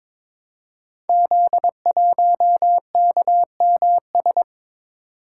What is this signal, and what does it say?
Morse code, heard as a keyed tone, about 22 wpm, 700 Hz. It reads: Z1KMS
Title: Code Groups - 5 Letter:Numbers - 22wpm